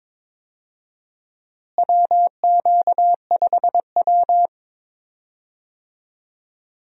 Morse code, heard as a keyed tone, about 22 wpm, 700 Hz. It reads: WQ5W
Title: USA Call Signs - 4 Characters - Repeated 2x - 22wpm